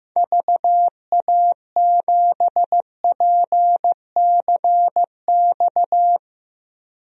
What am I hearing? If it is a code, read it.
Morse code, heard as a keyed tone, about 15 wpm, 700 Hz. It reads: VA7PCX
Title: DXCC-100 Call Signs - 6 Characters - Repeated 3x - 15wpm